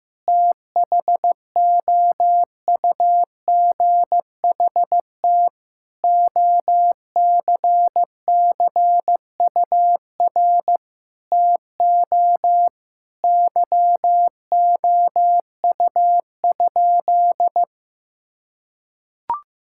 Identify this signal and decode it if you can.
Morse code, heard as a keyed tone, about 15 wpm, 700 Hz. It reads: THOUGHT OCCUR TO YOU?
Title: Sentences from Top 1000 Words - Review Part 6 - 15wpm